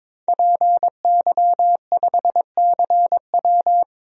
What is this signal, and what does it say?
Morse code, heard as a keyed tone, about 22 wpm, 700 Hz. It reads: PY5CW
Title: DXCC-100 Call Signs - 5 Characters - 22wpm